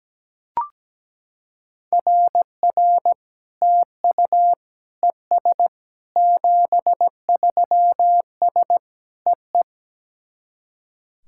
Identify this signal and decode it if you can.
Morse code, heard as a keyed tone, about 18 wpm, 700 Hz. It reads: RR TU ES 73S EE
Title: SOTA QSO Practice 17wpm